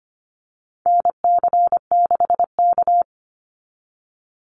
Morse code, heard as a keyed tone, about 25 wpm, 700 Hz. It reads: NC6K
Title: Call Signs - 25wpm